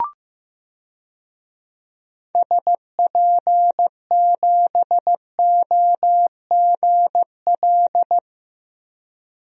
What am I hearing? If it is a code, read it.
Morse code, heard as a keyed tone, about 15 wpm, 700 Hz. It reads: SP7OGL
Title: DXCC-100 Call Signs - 6 Characters - Repeated 2x - 15wpm